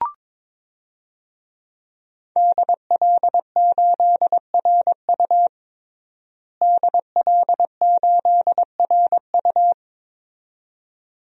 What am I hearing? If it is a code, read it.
Morse code, heard as a keyed tone, about 22 wpm, 700 Hz. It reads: DL8RU DL8RU
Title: DXCC-100 Call Signs - 5 Characters - Repeated 2x - 22wpm